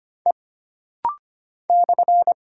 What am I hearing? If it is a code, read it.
Morse code, heard as a keyed tone, about 25 wpm, 700 Hz. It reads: E /
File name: Single Character (Letter:Number:Punctuation:Prosign) - Rapid-Fire - 25wpm